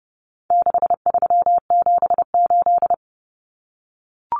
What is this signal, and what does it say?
Morse code, heard as a keyed tone, about 29 wpm, 700 Hz. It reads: 6378
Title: Numbers - 4-Digit 30wpm